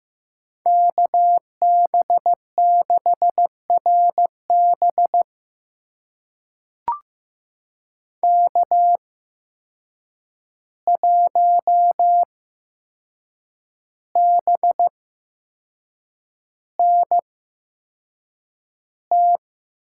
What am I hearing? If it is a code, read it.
Morse code, heard as a keyed tone, about 15 wpm, spaced as for 6 wpm, 700 Hz. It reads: KB6RB K1BNT
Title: USA Call Signs - 5 Characters - 8x Character Spacing - 15wpm